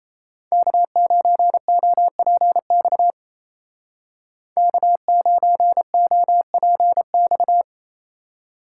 Morse code, heard as a keyed tone, about 33 wpm, 700 Hz. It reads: K9OPX K9OPX
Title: Speed Racing - USA 1x3 Call Signs 28wpm